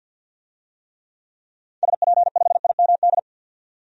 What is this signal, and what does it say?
Morse code, heard as a keyed tone, about 47 wpm, 700 Hz. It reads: SP5IMD
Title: DXCC-100 Call Signs - 6 Characters - 50wpm